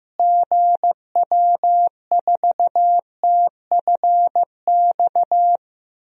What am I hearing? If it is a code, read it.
Morse code, heard as a keyed tone, about 15 wpm, 700 Hz. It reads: GW4TFX
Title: DXCC-100 Call Signs - 6 Characters - Repeated 2x - 15wpm